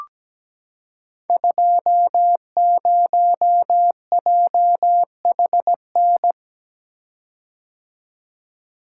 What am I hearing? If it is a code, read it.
Morse code, heard as a keyed tone, about 17 wpm, 700 Hz. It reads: 20JHN
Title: Code Groups - 5 Letter:Numbers - 17wpm